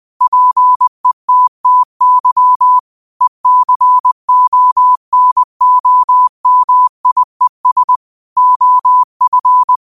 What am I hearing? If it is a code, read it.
Morse code, heard as a keyed tone, about 20 wpm, 1000 Hz. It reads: PETTY ECONOMIES OF
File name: qotdfiles / 20wpm